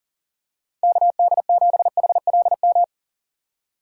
Morse code, heard as a keyed tone, about 37 wpm, 700 Hz. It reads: KD7HLM
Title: USA Call Signs - 6 Characters - 2x Character Spacing - 40wpm